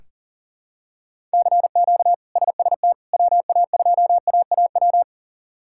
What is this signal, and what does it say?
Morse code, heard as a keyed tone, about 40 wpm, 700 Hz. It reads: CQ SST WA2AAW
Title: K1USN SST QSO Practice - 40wpm